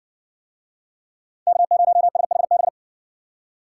Morse code, heard as a keyed tone, about 60 wpm, 700 Hz. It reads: K0SHB